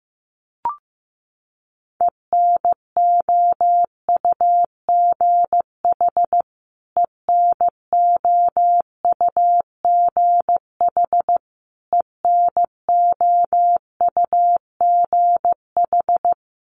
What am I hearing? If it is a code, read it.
Morse code, heard as a keyed tone, about 15 wpm, 700 Hz. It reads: ENOUGH ENOUGH ENOUGH
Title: Top 500 Words Repeated 15wpm